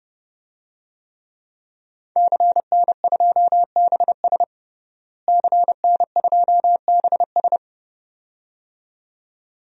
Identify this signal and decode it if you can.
Morse code, heard as a keyed tone, about 31 wpm, 700 Hz. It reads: CN2BS CN2BS
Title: DXCC-100 Call Signs - 5 Characters - Repeated 2x - 30wpm